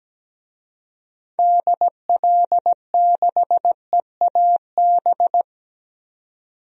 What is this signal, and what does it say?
Morse code, heard as a keyed tone, about 17 wpm, 700 Hz. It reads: DL6EAB
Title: DXCC-100 Call Signs - 6 Characters - 17wpm